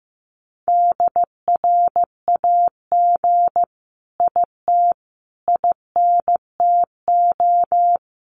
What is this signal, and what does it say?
Morse code, heard as a keyed tone, about 15 wpm, 700 Hz. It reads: DRAG IT INTO
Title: Sentences from Top 1500 Words 15wpm